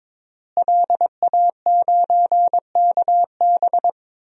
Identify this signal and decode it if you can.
Morse code, heard as a keyed tone, about 21 wpm, 700 Hz. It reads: LA9KB
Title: DXCC-100 Call Signs - 5 Characters - Repeated 3x - 22wpm